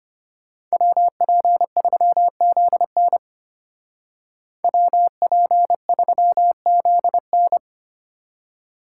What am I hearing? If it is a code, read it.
Morse code, heard as a keyed tone, about 30 wpm, 700 Hz. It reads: WP3ZN WP3ZN
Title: Speed Racing - USA Call Signs 25wpm